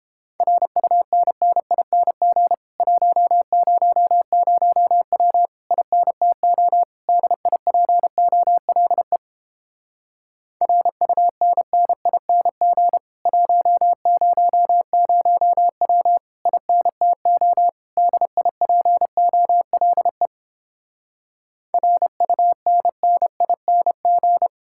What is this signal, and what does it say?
Morse code, heard as a keyed tone, about 29 wpm, 700 Hz. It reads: RUNNING 100W INTO DIPOLE RUNNING 100W INTO DIPOLE RUNNING